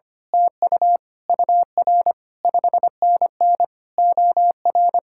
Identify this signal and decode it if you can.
Morse code, heard as a keyed tone, about 24 wpm, 700 Hz. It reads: TU UR 5NN OR
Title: Speed-Racing Common QSO Phrases Encore - 25wpm